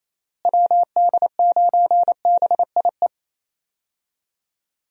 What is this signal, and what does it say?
Morse code, heard as a keyed tone, about 27 wpm, 700 Hz. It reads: WD9BIE
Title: USA Call Signs - 6 Characters - Repeated 3x - 28wpm